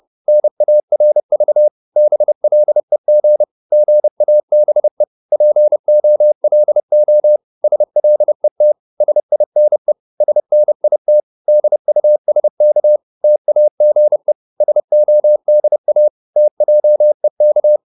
Morse code, heard as a keyed tone, about 30 wpm, 600 Hz. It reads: NARV BLEG GABE POLO SLET SINE SNIT DUSK TAGE SODA TJEK
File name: Korte ord DK 30wpm